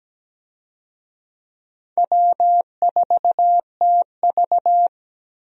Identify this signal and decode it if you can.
Morse code, heard as a keyed tone, about 17 wpm, 700 Hz. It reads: W4TV
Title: USA Call Signs - 4 Characters - Repeated 2x - 17wpm